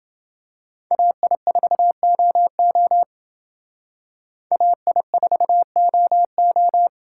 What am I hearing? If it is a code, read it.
Morse code, heard as a keyed tone, about 32 wpm, 700 Hz. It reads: AI4OO AI4OO